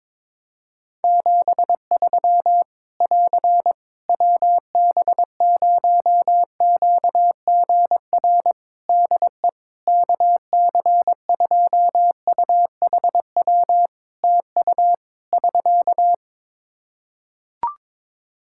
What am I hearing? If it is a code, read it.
Morse code, heard as a keyed tone, about 22 wpm, 700 Hz. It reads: 73 <AR> WB0QGR DE KC2UHW TU <SK>
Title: Practice QSO - Exchange 4 - Ending - 22wpm